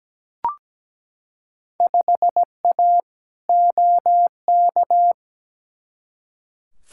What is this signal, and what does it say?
Morse code, heard as a keyed tone, about 17 wpm, 700 Hz. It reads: 5A OK
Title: Field Day Exchanges - 17wpm